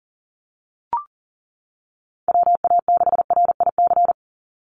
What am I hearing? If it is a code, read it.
Morse code, heard as a keyed tone, about 40 wpm, 700 Hz. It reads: WA6RIC
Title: NA SOTA Chaser Callsigns 40wpm